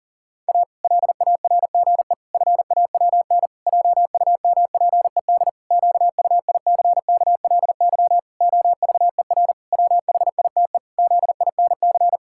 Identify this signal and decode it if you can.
Morse code, heard as a keyed tone, about 40 wpm, 700 Hz. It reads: A LARGE FAWN JUMPED QUICKLY OVER WHITE ZINC
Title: Pangram Challenge v2 - 40wpm